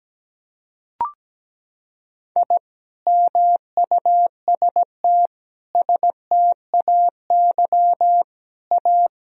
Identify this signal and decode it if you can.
Morse code, heard as a keyed tone, about 17 wpm, 700 Hz. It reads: I MUST STAY A
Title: Sentences from Top 1000 Words - Review Part 1 - 17wpm